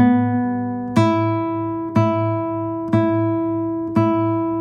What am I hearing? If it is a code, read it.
There is no code.